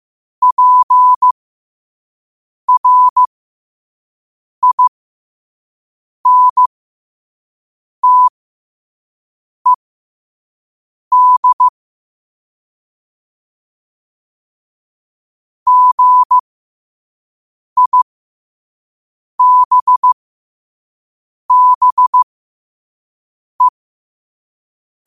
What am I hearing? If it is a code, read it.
Morse code, heard as a keyed tone, about 15 wpm, spaced as for 5 wpm, 1000 Hz. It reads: PRINTED GIBBE
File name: qotdfiles / 5wpm